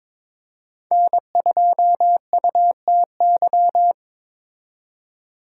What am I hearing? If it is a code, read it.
Morse code, heard as a keyed tone, about 26 wpm, 700 Hz. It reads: N2UTY
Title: Speed Racing - USA 1x3 Call Signs 22wpm